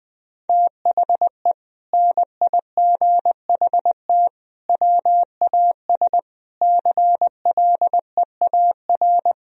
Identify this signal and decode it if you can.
Morse code, heard as a keyed tone, about 21 wpm, 700 Hz. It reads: THE NIGHT WAS CLEAR
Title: Sentences from Top 1000 Words - Review Part 2 - 20wpm